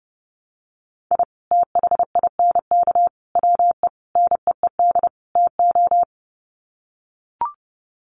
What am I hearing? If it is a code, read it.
Morse code, heard as a keyed tone, about 30 wpm, 700 Hz. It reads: I THINK WE NEED TO
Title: Sets of 5 Words 30wpm